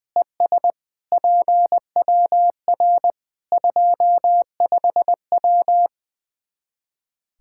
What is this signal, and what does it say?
Morse code, heard as a keyed tone, about 20 wpm, 700 Hz. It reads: ES PWR 25W
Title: Practice QSO - Exchange 2 - Info - 20wpm